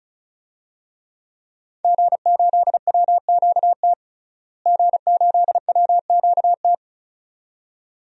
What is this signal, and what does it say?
Morse code, heard as a keyed tone, about 36 wpm, 700 Hz. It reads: G8WQT G8WQT
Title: DXCC-100 Call Signs - 5 Characters - Repeated 2x - 35wpm